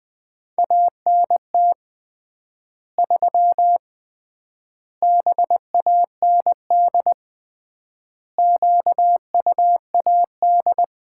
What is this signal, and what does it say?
Morse code, heard as a keyed tone, about 20 wpm, 700 Hz. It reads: ANT 3 BAND QUAD
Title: Common QSO Phrases - 3x Word-Spacing - 20wpm